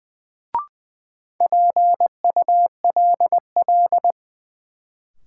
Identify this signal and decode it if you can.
Morse code, heard as a keyed tone, about 20 wpm, 700 Hz. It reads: PULL
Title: Sentences from Top 400 Words - New Word Drill - 20wpm